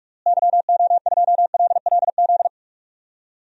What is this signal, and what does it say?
Morse code, heard as a keyed tone, about 46 wpm, 700 Hz. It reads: YO2LLZ
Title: DXCC-100 Call Signs - 6 Characters - Repeated 2x - 45wpm